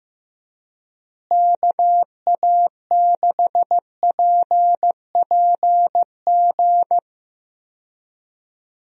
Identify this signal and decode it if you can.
Morse code, heard as a keyed tone, about 15 wpm, 700 Hz. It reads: KA6PPG
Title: USA Call Signs - 6 Characters - Repeated 2x - 15wpm